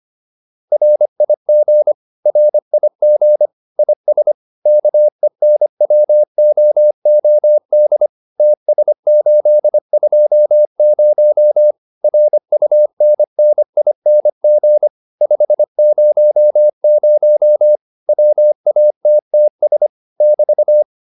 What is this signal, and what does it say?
Morse code, heard as a keyed tone, about 25 wpm, 600 Hz. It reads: RIG RIG IS KENWOOD TS820 RUNNING 500 WATTS <BT>
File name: QSO408a 25